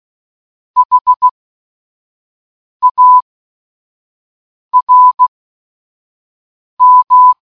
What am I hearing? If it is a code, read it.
Morse code, heard as a keyed tone, about 16 wpm, spaced as for 5 wpm, 1000 Hz. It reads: HARM
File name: escaperoom-019-morsecode-charmander